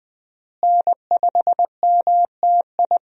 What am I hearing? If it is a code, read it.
Morse code, heard as a keyed tone, about 20 wpm, 700 Hz. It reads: N5MTI
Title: USA Call Signs - 5 Characters - Repeated 3x - 20wpm